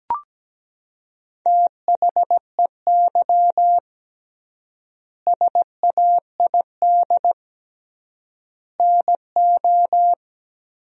Morse code, heard as a keyed tone, about 17 wpm, 700 Hz. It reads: THEY SAID NO
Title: 3x Word-Spacing - Sentences from Top 100 Words - 17wpm